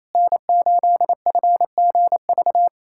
Morse code, heard as a keyed tone, about 28 wpm, 700 Hz. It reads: N8FGV
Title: USA Callsign DE Callsign - 5 Characters - 28wpm